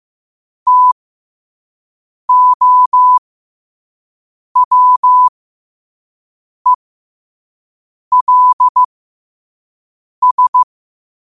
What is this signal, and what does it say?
Morse code, heard as a keyed tone, about 15 wpm, spaced as for 5 wpm, 1000 Hz. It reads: TOWELS
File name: qotdfiles / 5wpm